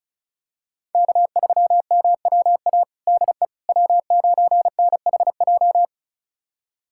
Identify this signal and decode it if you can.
Morse code, heard as a keyed tone, about 34 wpm, 700 Hz. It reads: K3MWA DE W9NHJ
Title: USA Callsign DE Callsign - 5 Characters - 35wpm